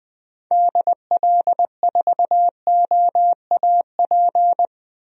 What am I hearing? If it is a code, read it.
Morse code, heard as a keyed tone, about 20 wpm, 700 Hz. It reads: DL4OAP
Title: USA Call Signs - 6 Characters - Repeated 3x - 20wpm